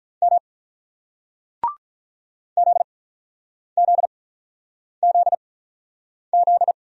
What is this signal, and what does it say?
Morse code, heard as a keyed tone, about 43 wpm, spaced as for 9 wpm, 700 Hz. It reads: M ZZZZ
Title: Speed-Racing Single Letters 35wpm